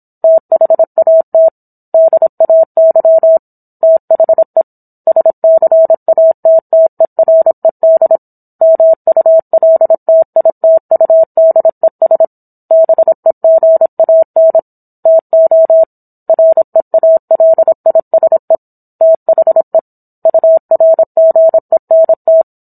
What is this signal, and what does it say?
Morse code, heard as a keyed tone, about 27 wpm, 650 Hz. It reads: THAT DAY THE SCATTERED MULTITUDES BEGAN TO REALISE THE URGENT